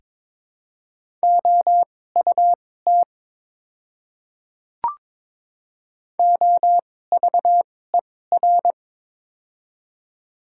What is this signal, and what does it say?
Morse code, heard as a keyed tone, about 22 wpm, spaced as for 16 wpm, 700 Hz. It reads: OUT OVER
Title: 2x Character Spacing - Top 100 Words - 22wpm